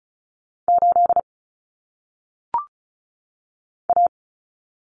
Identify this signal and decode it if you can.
Morse code, heard as a keyed tone, about 36 wpm, 700 Hz. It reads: 8 A
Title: Common Cut Numbers - 1-Digit 35wpm